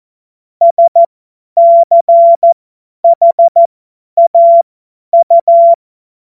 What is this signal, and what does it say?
Morse code, heard as a keyed tone, about 14 wpm, spaced as for 10 wpm, 680 Hz. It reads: SCHAU
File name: Morsekurs / Witze / Witz 27. Juli 2021, 50 F70